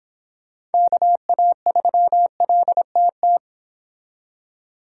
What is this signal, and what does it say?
Morse code, heard as a keyed tone, about 28 wpm, 700 Hz. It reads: KA3LTT